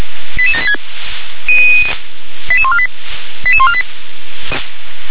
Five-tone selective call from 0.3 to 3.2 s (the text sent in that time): 88786 88148